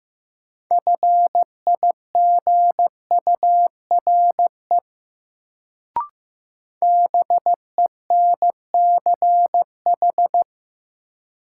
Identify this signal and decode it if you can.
Morse code, heard as a keyed tone, about 15 wpm, 700 Hz. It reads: FIGURE BENCH
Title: Sentences from Top 1400 Words - New Word Drill Alt - 15wpm